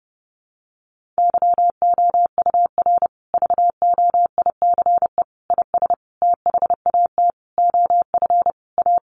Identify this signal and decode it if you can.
Morse code, heard as a keyed tone, about 28 wpm, 700 Hz. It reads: YOUR VOICE IS THAT OF A
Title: Sentences from Top 500 Words 30wpm